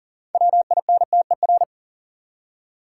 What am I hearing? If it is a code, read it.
Morse code, heard as a keyed tone, about 40 wpm, 700 Hz. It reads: WINTER
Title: Six Letter Words - 40wpm